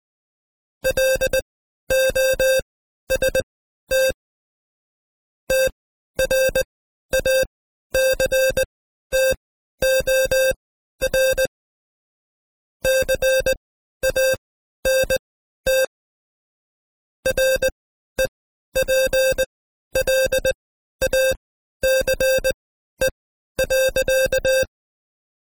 Morse code, heard as a keyed tone, about 19 wpm, spaced as for 12 wpm, 530 Hz. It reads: LOST TRACTOR CANT REPLACE.
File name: Code Quick / AudioDownld2 / Seg132-Stepper5 11WPM